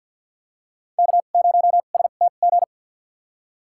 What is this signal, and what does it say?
Morse code, heard as a keyed tone, about 53 wpm, spaced as for 37 wpm, 700 Hz. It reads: K0STG
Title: USA Call Signs - 5 Characters - 2x Character Spacing - 50wpm